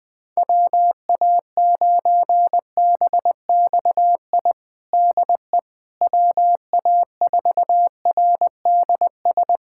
Morse code, heard as a keyed tone, about 19 wpm, 700 Hz. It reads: WA9BXI DE WA4RDS
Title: Practice QSO - Exchange 4 - Ending - 20wpm